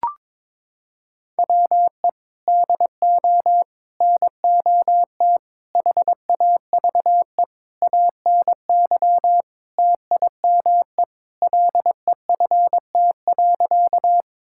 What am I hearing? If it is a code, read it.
Morse code, heard as a keyed tone, about 22 wpm, 700 Hz. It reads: WE DO NOT HAVE ANY TIME LEFT.
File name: practice / 22wpm / Sentences with 4-Word Phrases - 22wpm